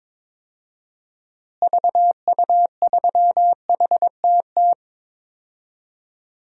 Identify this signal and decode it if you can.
Morse code, heard as a keyed tone, about 21 wpm, 700 Hz. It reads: VU3HTT
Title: DXCC-100 Call Signs - 6 Characters - Repeated 2x - 22wpm